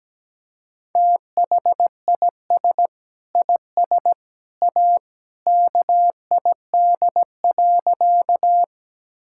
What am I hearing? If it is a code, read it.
Morse code, heard as a keyed tone, about 17 wpm, 700 Hz. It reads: THIS IS A KID.
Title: Sentences from Top 1600 Words 17wpm